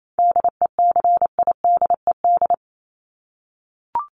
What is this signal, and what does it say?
Morse code, heard as a keyed tone, about 28 wpm, 700 Hz. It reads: DECIDED
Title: ED Suffix - Top 100 Words with Base Words 28wpm